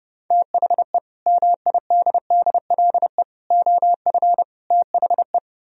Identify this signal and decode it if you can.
Morse code, heard as a keyed tone, about 38 wpm, 700 Hz. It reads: THE MIDDLE OF THE